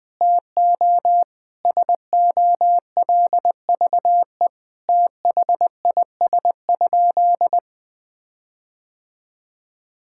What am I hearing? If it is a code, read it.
Morse code, heard as a keyed tone, about 20 wpm, 700 Hz. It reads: TO SOLVE THIS?
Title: Sentences with 5-Word Phrases - 20wpm